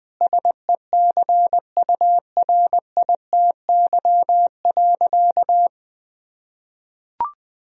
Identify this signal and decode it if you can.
Morse code, heard as a keyed tone, about 20 wpm, 700 Hz. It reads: SECURITY.
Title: Sentences from Top 1000 Words - Review Part 4 - 20wpm